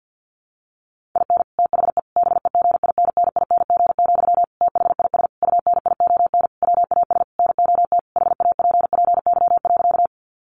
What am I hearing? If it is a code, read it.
Morse code, heard as a keyed tone, about 50 wpm, 700 Hz. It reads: IN THE BEGINNING, THIS UNION WAS NOT HAPPY.
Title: Sentences from Top 1400 Words 50wpm